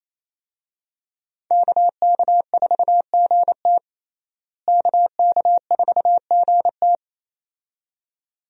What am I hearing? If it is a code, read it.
Morse code, heard as a keyed tone, about 28 wpm, 700 Hz. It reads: KK4GT KK4GT
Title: USA Call Signs - 5 Characters - Repeated 2x - 28wpm